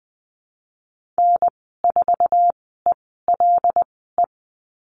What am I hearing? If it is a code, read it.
Morse code, heard as a keyed tone, about 20 wpm, spaced as for 15 wpm, 700 Hz. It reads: N4ELE
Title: 076 Lesson 13g - Letter L - Call Signs 20wpm